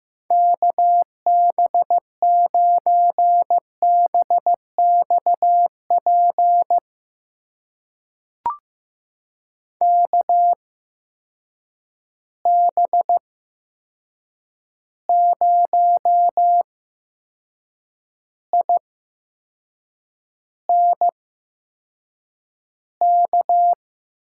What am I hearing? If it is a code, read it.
Morse code, heard as a keyed tone, about 15 wpm, spaced as for 6 wpm, 700 Hz. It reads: KB9BXP KB0INK
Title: USA Call Signs - 6 Characters - 8x Character Spacing - 15wpm